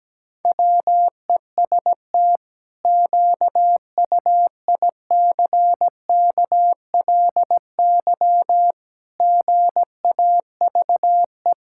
Morse code, heard as a keyed tone, about 17 wpm, 700 Hz. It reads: WEST QUICKLY GAVE
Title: Pangram Challenge v2 - 17wpm